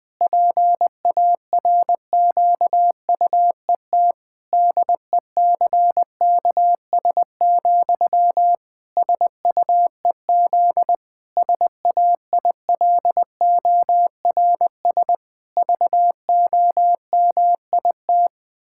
Morse code, heard as a keyed tone, about 20 wpm, 700 Hz. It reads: PARQUET DECKS, SUEZ SAILORS VOMIT
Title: Pangram Challenge v2 - 20wpm